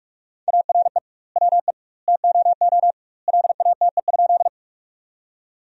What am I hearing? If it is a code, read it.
Morse code, heard as a keyed tone, about 46 wpm, 700 Hz. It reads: ARE WE TOO LATE?
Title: 2x Word-Spacing - Sentences from Top 300 Words - 45wpm